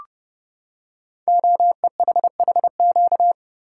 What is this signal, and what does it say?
Morse code, heard as a keyed tone, about 30 wpm, 700 Hz. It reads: OEHHQ
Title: Code Groups - 5 Letters - 30wpm